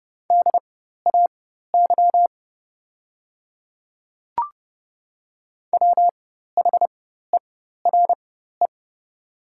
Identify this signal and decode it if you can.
Morse code, heard as a keyed tone, about 30 wpm, spaced as for 14 wpm, 700 Hz. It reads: DAY WHERE
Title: 4x Character Spacing - Top 200 Words - 30wpm